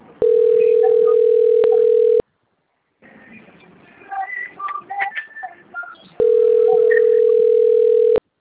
Ringback tone (North America), 440 + 480 Hz.
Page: Sao Carlos